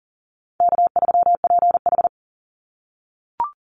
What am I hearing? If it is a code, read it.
Morse code, heard as a keyed tone, about 40 wpm, 700 Hz. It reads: K3PH
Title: Call Signs - 40wpm